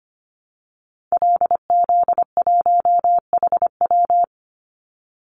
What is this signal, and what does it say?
Morse code, heard as a keyed tone, about 25 wpm, 700 Hz. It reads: LZ1HW
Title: Call Signs Repeated 2x - 25wpm